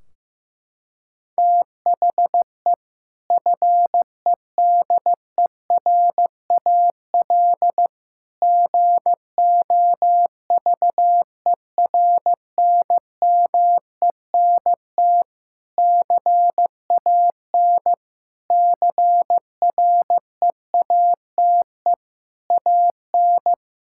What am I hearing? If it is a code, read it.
Morse code, heard as a keyed tone, about 15 wpm, 700 Hz. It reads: THE FEDERAL GOVERNMENT CAN CREATE AN
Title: Sentences from Top 1600 Words 15wpm